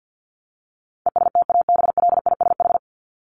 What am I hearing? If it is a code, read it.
Morse code, heard as a keyed tone, about 50 wpm, 700 Hz. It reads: ESTABLISH
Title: ED Suffix - Top 100 Words with Base Words 50wpm